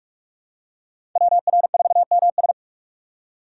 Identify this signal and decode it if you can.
Morse code, heard as a keyed tone, about 42 wpm, 700 Hz. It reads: WR4MS
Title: USA Call Signs - 5 Characters - 3x Character Spacing - 45wpm